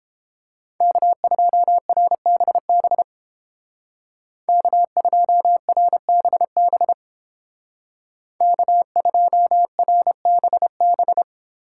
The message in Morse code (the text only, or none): K2RBB K2RBB K2RBB